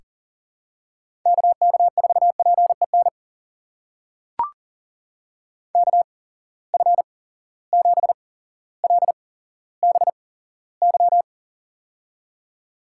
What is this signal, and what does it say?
Morse code, heard as a keyed tone, about 40 wpm, spaced as for 17 wpm, 700 Hz. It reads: KK4PEN KF7LBY
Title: USA Call Signs - 6 Characters - 8x Character Spacing - 40wpm